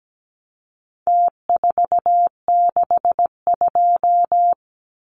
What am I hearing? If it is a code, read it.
Morse code, heard as a keyed tone, about 17 wpm, 700 Hz. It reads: T462
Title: Numbers - Contest Style - 4-Digit 17wpm